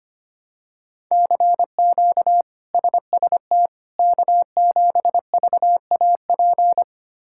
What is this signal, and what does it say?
Morse code, heard as a keyed tone, about 25 wpm, 700 Hz. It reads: CQ SST K7VAP
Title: K1USN SST QSO Practice - 25wpm